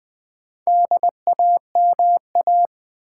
Morse code, heard as a keyed tone, about 20 wpm, 700 Hz. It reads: DAMA
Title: Build A Word - 2 Letters At A Time - 6-Letter Words - 20wpm